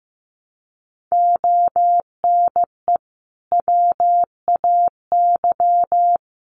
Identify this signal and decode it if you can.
Morse code, heard as a keyed tone, about 15 wpm, 700 Hz. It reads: ONE WAY
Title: Sets of 5 Words 15wpm